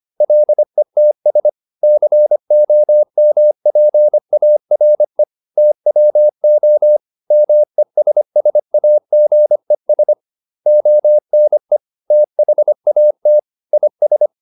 Morse code, heard as a keyed tone, about 25 wpm, 600 Hz. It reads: LETS COMPARE TWO MESSAGES ONE THAT IS